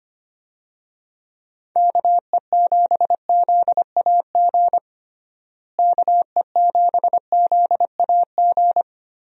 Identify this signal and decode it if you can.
Morse code, heard as a keyed tone, about 25 wpm, 700 Hz. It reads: KE7ZAG KE7ZAG
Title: USA Call Signs - 6 Characters - Repeated 3x - 25wpm